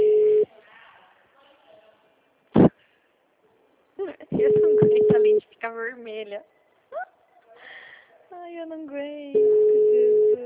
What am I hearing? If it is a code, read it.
Ringback tone (Europe, 425 Hz), with speech.
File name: out- -2022- - - .7953